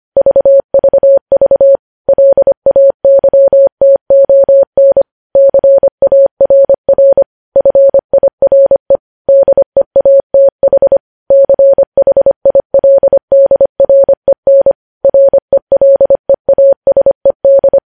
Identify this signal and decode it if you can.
Morse code, heard as a keyed tone, about 25 wpm, 550 Hz. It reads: VVV LAYTON CARR FIRE DEATH CHILDREN RELEASED
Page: News Headlines in Morse Code at 25 WPM